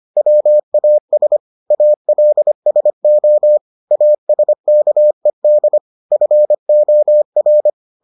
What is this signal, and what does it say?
Morse code, heard as a keyed tone, about 25 wpm, 600 Hz. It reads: WAS ALSO ASKED FOR